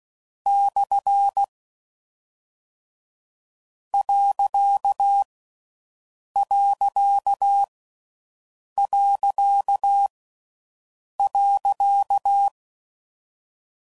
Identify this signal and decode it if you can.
Morse code, heard as a keyed tone, about 16 wpm, 1350 Hz. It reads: / . . .
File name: Mixed / Misc / Punct